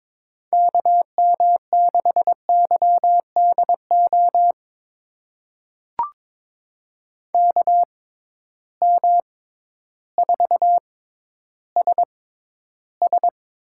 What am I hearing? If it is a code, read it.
Morse code, heard as a keyed tone, about 21 wpm, spaced as for 12 wpm, 700 Hz. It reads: KM6YDO KM4SS
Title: USA Call Signs - 6 Characters - 6x Character Spacing - 22wpm